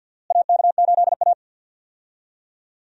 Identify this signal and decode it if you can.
Morse code, heard as a keyed tone, about 47 wpm, 700 Hz. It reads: AK8A
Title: USA Call Signs - 4 Characters - Repeated 3x - 50wpm